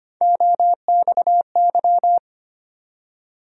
Morse code, heard as a keyed tone, about 25 wpm, 700 Hz. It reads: OXY
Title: USA Call Sign Postfixes - 3 Letters 25wpm